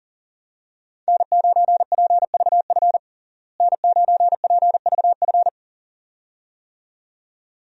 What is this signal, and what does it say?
Morse code, heard as a keyed tone, about 40 wpm, 700 Hz. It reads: N9PVF N9PVF
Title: USA Call Signs - 5 Characters - Repeated 2x - 40wpm